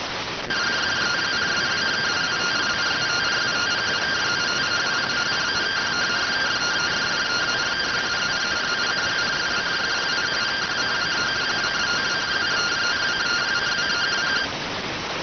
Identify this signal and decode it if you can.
Two-tone teleprinter signal, about 170 Hz shift.